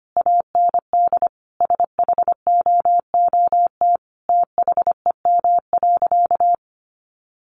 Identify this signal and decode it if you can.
Morse code, heard as a keyed tone, about 25 wpm, 700 Hz. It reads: AND SHOOT THEM.
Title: Sentences from Top 1200 Words 25wpm